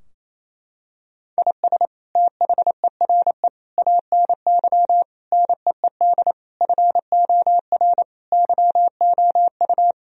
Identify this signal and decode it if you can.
Morse code, heard as a keyed tone, about 28 wpm, 700 Hz. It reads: IS THERE ANY NEED FOR YOU
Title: Sentences from Top 1000 Words - Review Part 4 - 28wpm